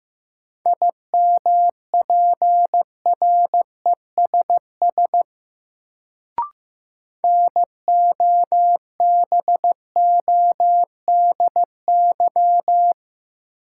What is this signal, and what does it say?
Morse code, heard as a keyed tone, about 15 wpm, 700 Hz. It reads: IMPRESS NOBODY
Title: Sentences from Top 1600 Words - New Word Drill Alt - 15wpm